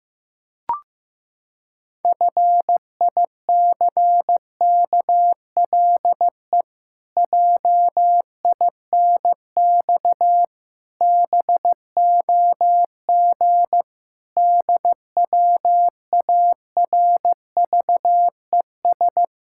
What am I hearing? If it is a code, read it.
Morse code, heard as a keyed tone, about 15 wpm, 700 Hz. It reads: FICKLE JINX BOG DWARVES
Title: Pangram Challenge v2 - 15wpm